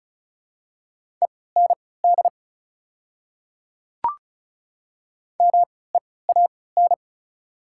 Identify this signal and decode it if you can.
Morse code, heard as a keyed tone, about 34 wpm, spaced as for 20 wpm, 700 Hz. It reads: END MEAN
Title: 3x Character Spacing - Top 200 Words - 35wpm